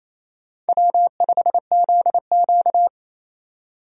Morse code, heard as a keyed tone, about 27 wpm, 700 Hz. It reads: W5ZQ
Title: USA Call Signs - 4 Characters - 28wpm